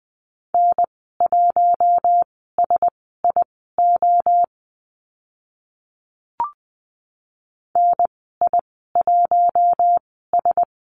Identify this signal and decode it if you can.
Morse code, heard as a keyed tone, about 20 wpm, spaced as for 14 wpm, 700 Hz. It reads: N1SIO NI1S
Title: 042 Lesson 8g - Number 1 - Call Signs 20wpm